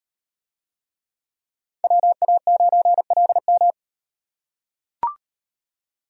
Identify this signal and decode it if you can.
Morse code, heard as a keyed tone, about 36 wpm, 700 Hz. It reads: WA9LM